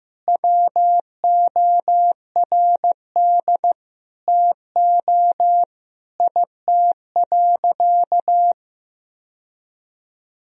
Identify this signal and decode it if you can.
Morse code, heard as a keyed tone, about 15 wpm, 700 Hz. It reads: WORD TO IT.
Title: Sentences from Top 1000 Words - Review Part 1 - 15wpm